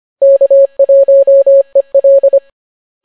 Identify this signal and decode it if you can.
Morse code, heard as a keyed tone, about 25 wpm, 550 Hz. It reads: K1EL